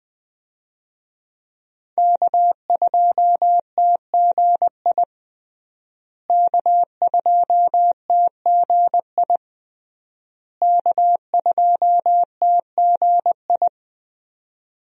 Morse code, heard as a keyed tone, about 20 wpm, 700 Hz. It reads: K2TGI K2TGI K2TGI
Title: USA Call Signs - 5 Characters - Repeated 3x - 20wpm